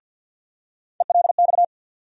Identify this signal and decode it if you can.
Morse code, heard as a keyed tone, about 51 wpm, 700 Hz. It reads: ELX
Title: USA Call Sign Postfixes - 3 Letters 50wpm